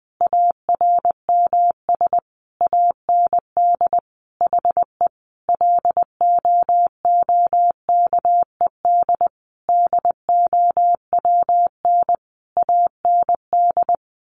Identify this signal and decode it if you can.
Morse code, heard as a keyed tone, about 20 wpm, 700 Hz. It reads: ARMS AND HE LOOKED DOWN AND
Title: A Sound Of Thunder 20wpm